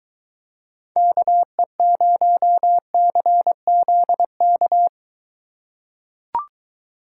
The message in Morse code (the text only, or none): KE0CZK